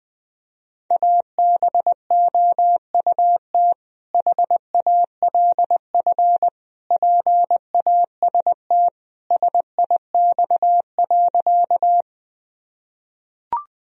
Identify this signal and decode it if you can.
Morse code, heard as a keyed tone, about 20 wpm, 700 Hz. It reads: ABOUT HALF PAST SIX.
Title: Sentences from Top 1000 Words - Review Part 3 - 20wpm